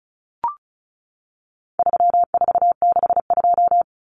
Morse code, heard as a keyed tone, about 34 wpm, 700 Hz. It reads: 3462
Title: Common Cut Numbers - 4-Digit 35wpm